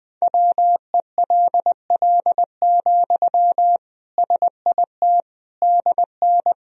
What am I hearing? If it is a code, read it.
Morse code, heard as a keyed tone, about 20 wpm, 700 Hz. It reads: WELL, SIT DN
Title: Sentences from Top 500 Words and Abbreviations 20wpm